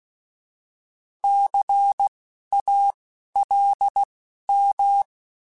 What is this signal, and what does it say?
Morse code, heard as a keyed tone, about 16 wpm, spaced as for 12 wpm, 800 Hz. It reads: CALM
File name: Alphabetical / Alphabetical mp3s / Cs @10